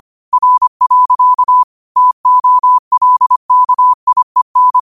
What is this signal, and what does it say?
Morse code, heard as a keyed tone, about 25 wpm, 1000 Hz. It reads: R. TOLKIEN
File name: qotdfiles / 25wpm